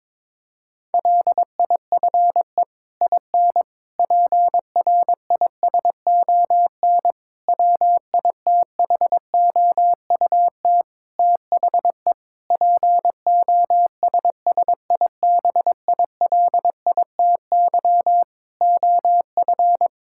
Morse code, heard as a keyed tone, about 22 wpm, 700 Hz. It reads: LIFE IN PRISON WITHOUT THE POSSIBILITY OF
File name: practice / 22wpm / Sets of 8 Words - 22wpm